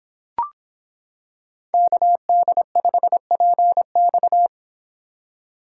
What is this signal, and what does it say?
Morse code, heard as a keyed tone, about 26 wpm, 700 Hz. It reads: KD5PX